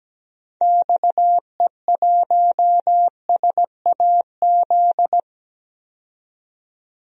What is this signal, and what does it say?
Morse code, heard as a keyed tone, about 17 wpm, 700 Hz. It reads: XE1SAZ
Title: DXCC-100 Call Signs - 6 Characters - 17wpm